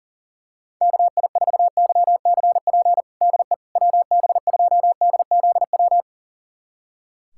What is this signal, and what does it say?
Morse code, heard as a keyed tone, about 40 wpm, 700 Hz. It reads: KI4YCP DE WB2DZW
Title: USA Callsign DE Callsign - 6 Characters - 40wpm